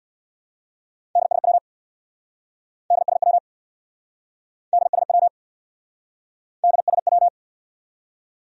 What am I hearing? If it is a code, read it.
Morse code, heard as a keyed tone, about 69 wpm, 700 Hz. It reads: DSW DSW DSW DSW